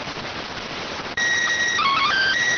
SSTV (slow-scan television), PD 290.